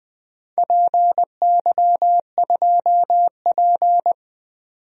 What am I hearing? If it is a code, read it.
Morse code, heard as a keyed tone, about 20 wpm, 700 Hz. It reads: PY2P
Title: DXCC-100 Call Signs - 4 Characters - 20wpm